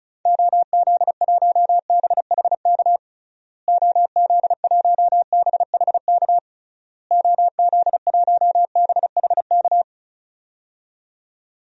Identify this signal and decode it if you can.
Morse code, heard as a keyed tone, about 36 wpm, 700 Hz. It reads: OZ1BHK OZ1BHK OZ1BHK
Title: DXCC-100 Call Signs - 6 Characters - Repeated 3x - 35wpm